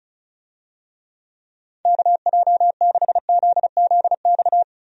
Morse code, heard as a keyed tone, about 36 wpm, 700 Hz. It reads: KJ6ZZX
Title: USA Call Signs - 6 Characters - Repeated 2x - 35wpm